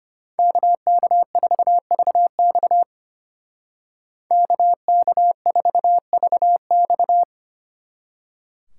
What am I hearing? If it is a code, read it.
Morse code, heard as a keyed tone, about 33 wpm, 700 Hz. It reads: KK4VX KK4VX